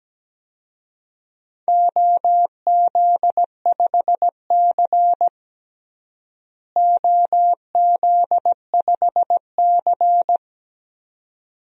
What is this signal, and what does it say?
Morse code, heard as a keyed tone, about 17 wpm, 700 Hz. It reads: OZ5C OZ5C
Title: DXCC-100 Call Signs - 4 Characters - Repeated 2x - 17wpm